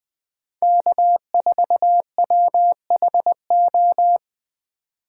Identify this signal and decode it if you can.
Morse code, heard as a keyed tone, about 20 wpm, 700 Hz. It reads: K4WHO
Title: USA Call Signs - 5 Characters - 3x Character Spacing - 20wpm